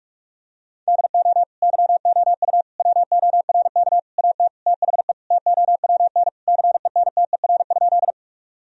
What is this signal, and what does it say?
Morse code, heard as a keyed tone, about 45 wpm, 700 Hz. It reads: DO YOU WORK AT THE TOWN CENTER?
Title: Sentences from Top 1000 Words - Review Part 2 - 45wpm